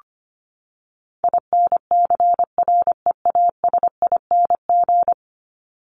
Morse code, heard as a keyed tone, about 25 wpm, 700 Hz. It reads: INCREASING
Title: ING Suffix - Top 100 Words with Base Words 25wpm